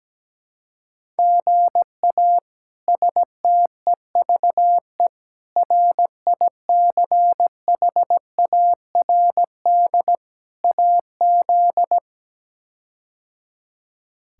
Morse code, heard as a keyed tone, about 17 wpm, 700 Hz. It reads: GA STEVE RICHARD AZ
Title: K1USN SST QSO Practice - 17wpm